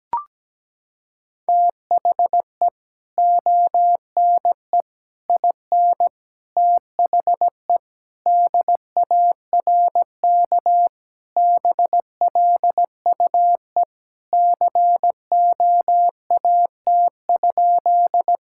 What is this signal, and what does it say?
Morse code, heard as a keyed tone, about 17 wpm, 700 Hz. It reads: THE ONE IN THE DARK BLUE COAT?
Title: Sentences from Top 1600 Words 17wpm